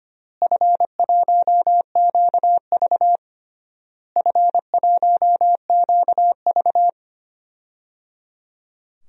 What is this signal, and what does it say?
Morse code, heard as a keyed tone, about 25 wpm, 700 Hz. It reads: F1QV F1QV
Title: DXCC-100 Call Signs - 4 Characters - Repeated 3x - 25wpm